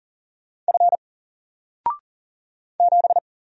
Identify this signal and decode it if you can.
Morse code, heard as a keyed tone, about 40 wpm, 700 Hz. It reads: F 7
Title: Single Letter-Number 40wpm